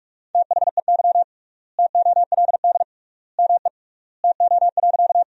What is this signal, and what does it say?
Morse code, heard as a keyed tone, about 46 wpm, 700 Hz. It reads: THEY TOLD ME TO.
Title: 3x Word-Spacing - Sentences from Top 400 Words - 45wpm